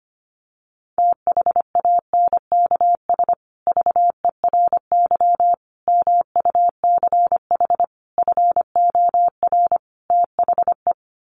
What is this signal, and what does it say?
Morse code, heard as a keyed tone, about 25 wpm, 700 Hz. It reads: THANKS VERY MUCH FOR THE
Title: Sets of 5 Words Encore2 25wpm